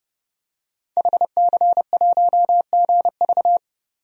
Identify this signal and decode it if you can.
Morse code, heard as a keyed tone, about 28 wpm, 690 Hz. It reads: HC1GV
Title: DXCC-100 Call Signs - 5 Characters - 30wpm